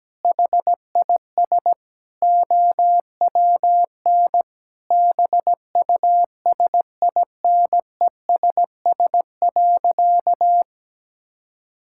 Morse code, heard as a keyed tone, about 17 wpm, 700 Hz. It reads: HIS OWN BUSINESS.
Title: Sentences from Top 1000 Words - Review Part 6 - 17wpm